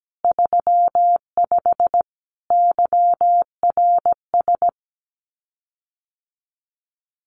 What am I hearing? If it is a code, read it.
Morse code, heard as a keyed tone, about 17 wpm, 700 Hz. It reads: 35 YRS
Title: Common QSO Phrases Encore 17wpm